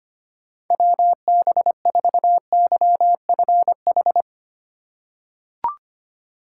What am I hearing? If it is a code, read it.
Morse code, heard as a keyed tone, about 25 wpm, 700 Hz. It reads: WB4YFH
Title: USA Call Signs - 6 Characters - 3x Character Spacing - 25wpm